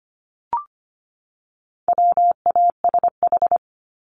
Morse code, heard as a keyed tone, about 25 wpm, 700 Hz. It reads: WASH
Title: Top 1751-2000 Words 25wpm